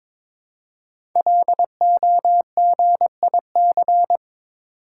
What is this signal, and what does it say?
Morse code, heard as a keyed tone, about 22 wpm, 700 Hz. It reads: LOGIC
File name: 5 Letter Words - First 3 Letters Given - 22wpm